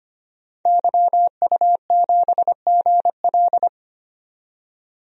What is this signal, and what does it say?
Morse code, heard as a keyed tone, about 25 wpm, 700 Hz. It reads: YU7GL
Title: DXCC-100 Call Signs - 5 Characters - 25wpm